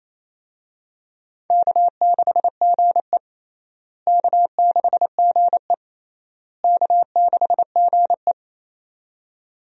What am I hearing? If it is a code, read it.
Morse code, heard as a keyed tone, about 28 wpm, 700 Hz. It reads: K6GE K6GE K6GE
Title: USA Call Signs - 4 Characters - Repeated 3x - 28wpm